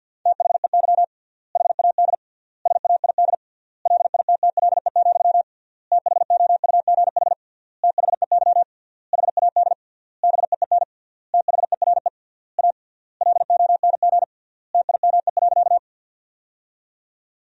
Morse code, heard as a keyed tone, about 50 wpm, 700 Hz. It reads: THEY HAD SAID LITTLE, THOUGH THEY HAD BEEN THERE A LONG TIME.
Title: 3x Word-Spacing - Sentences from Top 400 Words - 50wpm